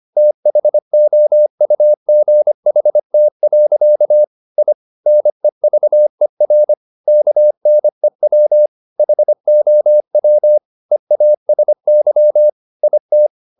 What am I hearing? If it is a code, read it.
Morse code, heard as a keyed tone, about 25 wpm, 600 Hz. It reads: THOUGHT. I NEVER KNEW HOW EASY IT